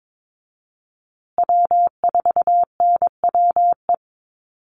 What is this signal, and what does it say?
Morse code, heard as a keyed tone, about 22 wpm, 700 Hz. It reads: W4NWE
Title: USA Call Signs Repeated 2x 22wpm